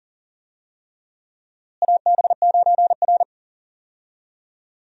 Morse code, heard as a keyed tone, about 40 wpm, 700 Hz. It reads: AB9R
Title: USA Call Signs - 4 Characters - Repeated 3x - 40wpm